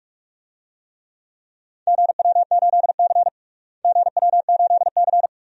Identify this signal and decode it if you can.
Morse code, heard as a keyed tone, about 46 wpm, 700 Hz. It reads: GW8C GW8C
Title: DXCC-100 Call Signs - 4 Characters - Repeated 2x - 45wpm